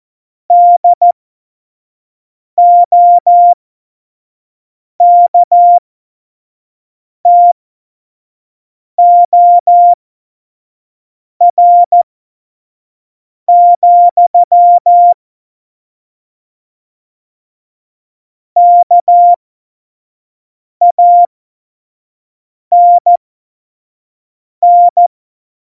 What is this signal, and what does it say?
Morse code, heard as a keyed tone, about 14 wpm, spaced as for 5 wpm, 700 Hz. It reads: DOKTOR, KANN
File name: Morsekurs / Witze / Witz 30. Januar 2018, 25 F70